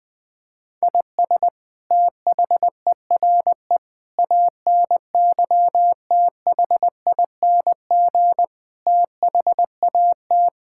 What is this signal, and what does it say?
Morse code, heard as a keyed tone, about 20 wpm, 700 Hz. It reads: IS THERE ANYTHING THAT
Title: Sentences from Top 1000 Words - Review Part 6 - 20wpm